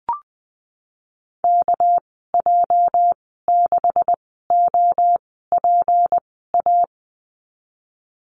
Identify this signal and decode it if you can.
Morse code, heard as a keyed tone, about 20 wpm, spaced as for 15 wpm, 700 Hz. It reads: KJ6OPA
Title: 238 Lesson 38g - Letter q - Call Signs 20wpm